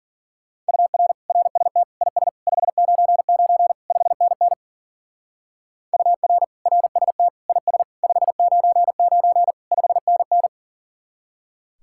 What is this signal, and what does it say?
Morse code, heard as a keyed tone, about 49 wpm, 700 Hz. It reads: UR RST IS 599 5NN UR RST IS 599 5NN
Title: Speed-Racing Common QSO Phrases - 40wpm